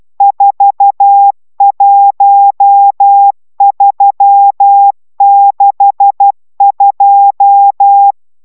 Morse code, heard as a keyed tone, about 12 wpm, 800 Hz. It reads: 41362
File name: taviro tanulas / cw 69 22 10